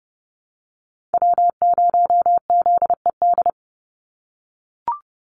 Morse code, heard as a keyed tone, about 30 wpm, 700 Hz. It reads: W0ZED
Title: USA Call Signs 30wpm